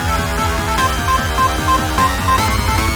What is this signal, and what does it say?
There is no code.